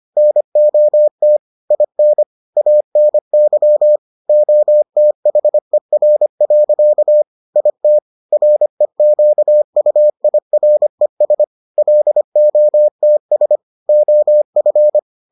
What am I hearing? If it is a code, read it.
Morse code, heard as a keyed tone, about 25 wpm, 600 Hz. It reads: NOT IN ANY OTHER. IT REQUIRES LOTS OF